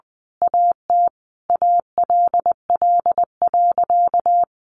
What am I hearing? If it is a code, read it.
Morse code, heard as a keyed tone, about 20 wpm, 700 Hz. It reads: AT ALL.
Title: Sentences from Top 1300 Words 20wpm